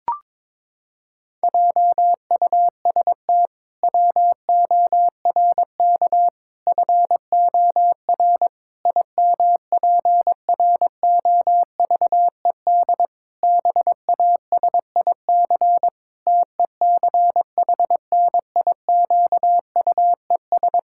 Morse code, heard as a keyed tone, about 21 wpm, 700 Hz. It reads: JUST WORK FOR IMPROVED BASIC TECHNIQUES
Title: Pangram Challenge v2 - 22wpm